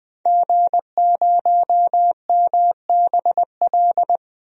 Morse code, heard as a keyed tone, about 20 wpm, 700 Hz. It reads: G0MBL
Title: DXCC-100 Call Signs - 5 Characters - 20wpm